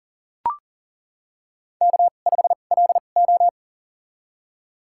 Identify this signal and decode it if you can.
Morse code, heard as a keyed tone, about 40 wpm, spaced as for 29 wpm, 700 Hz. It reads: K5LO
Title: USA Call Signs - 2x Character Spacing - 40wpm